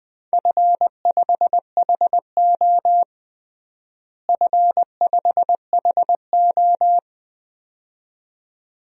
Morse code, heard as a keyed tone, about 20 wpm, 700 Hz. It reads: F5HO F5HO
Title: DXCC-100 Call Signs - 4 Characters - Repeated 2x - 20wpm